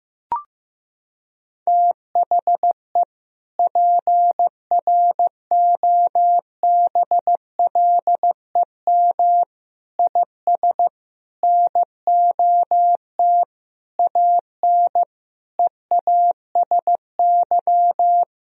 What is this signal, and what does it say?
Morse code, heard as a keyed tone, about 15 wpm, 700 Hz. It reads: THE PROBLEM IS NOT AN EASY
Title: Sentences from Top 1000 Words - Review Part 2 - 15wpm